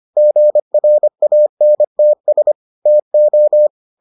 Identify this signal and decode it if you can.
Morse code, heard as a keyed tone, about 25 wpm, 600 Hz. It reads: GRANTS TO